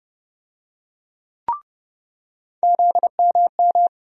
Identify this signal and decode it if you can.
Morse code, heard as a keyed tone, about 31 wpm, 700 Hz. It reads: ZMM
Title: USA Call Sign Postfixes - 3 Letters 30wpm